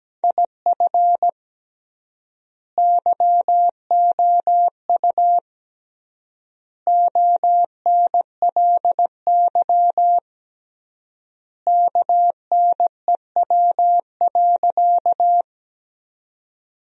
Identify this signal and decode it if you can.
Morse code, heard as a keyed tone, about 17 wpm, 700 Hz. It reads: IF YOU ONLY KNEW.
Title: 3x Word-Spacing - Sentences from Top 400 Words - 17wpm